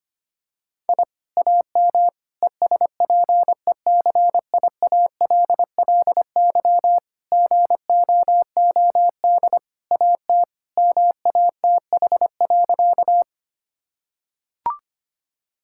Morse code, heard as a keyed tone, about 25 wpm, 700 Hz. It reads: I AM ESPECIALLY GOOD AT MATH.
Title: Sentences from Top 1000 Words - Review Part 6 - 25wpm